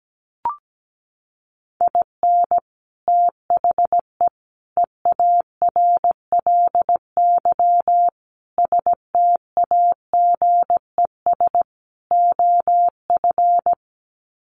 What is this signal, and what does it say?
Morse code, heard as a keyed tone, about 17 wpm, 700 Hz. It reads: IN THE EARLY STAGES OF
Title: Sets of 5 Words Encore 17wpm